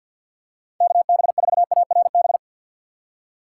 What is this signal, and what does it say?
Morse code, heard as a keyed tone, about 53 wpm, 700 Hz. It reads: KB4ARB
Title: USA Call Signs - 6 Characters - 6x Character Spacing - 50wpm